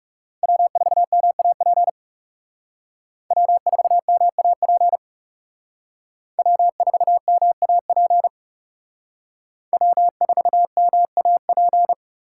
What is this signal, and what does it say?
Morse code, heard as a keyed tone, about 39 wpm, 700 Hz. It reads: W4MAP W4MAP W4MAP W4MAP